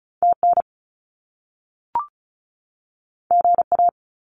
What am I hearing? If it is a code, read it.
Morse code, heard as a keyed tone, about 36 wpm, 700 Hz. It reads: TN GA
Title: US States 35wpm